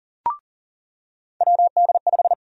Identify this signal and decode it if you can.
Morse code, heard as a keyed tone, about 40 wpm, 700 Hz. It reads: WD5
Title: USA Call Sign Prefixes - 40wpm